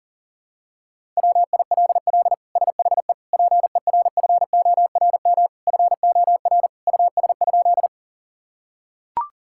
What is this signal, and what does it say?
Morse code, heard as a keyed tone, about 40 wpm, 700 Hz. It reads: WILL SHE PERFORM FOR US?
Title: Sentences from Top 1000 Words - Review Part 5 - 40wpm